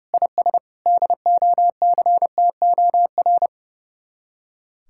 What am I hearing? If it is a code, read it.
Morse code, heard as a keyed tone, about 38 wpm, 700 Hz. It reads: IS DOCTOR
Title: Speed-Racing Common QSO Phrases - 30wpm